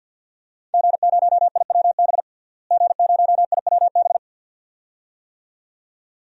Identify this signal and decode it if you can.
Morse code, heard as a keyed tone, about 45 wpm, 700 Hz. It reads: G0IWB G0IWB
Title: DXCC-100 Call Signs - 5 Characters - Repeated 2x - 50wpm